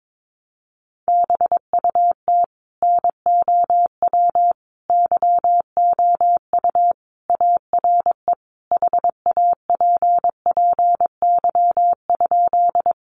Morse code, heard as a keyed tone, about 22 wpm, 700 Hz. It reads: BUT NOW YOU ARE HAPPY?
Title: Sentences from Top 1000 Words 22wpm